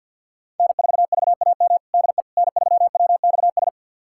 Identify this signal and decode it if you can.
Morse code, heard as a keyed tone, about 52 wpm, 700 Hz. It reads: N4VAM DE N3WXS
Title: USA Callsign DE Callsign - 5 Characters - 50wpm